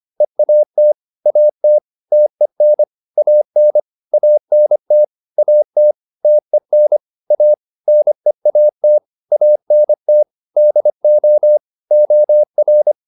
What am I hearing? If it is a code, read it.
Morse code, heard as a keyed tone, about 25 wpm, 600 Hz. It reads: EAT AT TEN AN ANT AT TEN A NEAT ANT DO OR